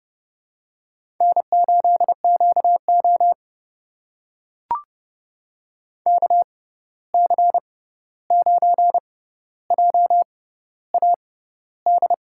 Morse code, heard as a keyed tone, about 32 wpm, spaced as for 14 wpm, 700 Hz. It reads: N8QO KC9JAD
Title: USA Call Signs - 6x Character Spacing - 30wpm